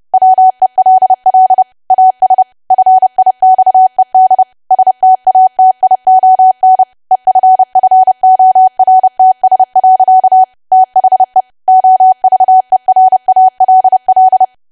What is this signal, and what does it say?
Morse code, heard as a keyed tone, about 30 wpm, 750 Hz. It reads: WELL AS FIXED STATION EFFORTS. THE OVERALL
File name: cwa / W1AW-Level3 / 30WPM